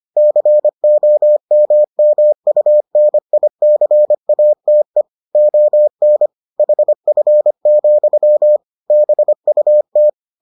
Morse code, heard as a keyed tone, about 25 wpm, 600 Hz. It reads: COMMUNICATE ON HF, BUT